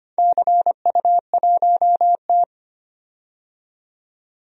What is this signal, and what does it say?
Morse code, heard as a keyed tone, about 25 wpm, 700 Hz. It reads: CU1T
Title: DXCC-100 Call Signs - 4 Characters - 25wpm